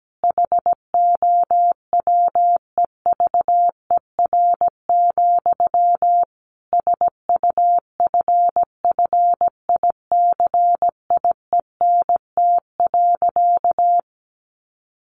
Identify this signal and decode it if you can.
Morse code, heard as a keyed tone, about 17 wpm, 700 Hz. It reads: HOWEVER, SUFFICIENT.
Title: Sentences from Top 1400 Words 17wpm